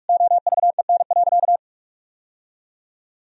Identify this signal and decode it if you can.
Morse code, heard as a keyed tone, about 46 wpm, 700 Hz. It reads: OVEN.
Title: Sentences from Top 1000 Words - Review Part 4 - 45wpm